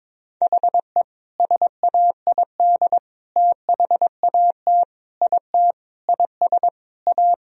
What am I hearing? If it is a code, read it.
Morse code, heard as a keyed tone, about 22 wpm, 700 Hz. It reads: HE SAID THAT IT IS A
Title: Sentences from Top 1000 Words - Review Part 4 - 22wpm